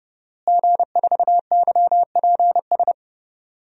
Morse code, heard as a keyed tone, about 30 wpm, 700 Hz. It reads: G4YPS
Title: DXCC-100 Call Signs - 5 Characters - Repeated 2x - 30wpm